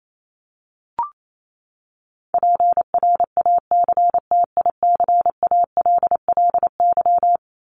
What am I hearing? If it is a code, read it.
Morse code, heard as a keyed tone, about 28 wpm, 700 Hz. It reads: PRACTICALLY
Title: LY Suffix - Top 100 Words with Base Words 28wpm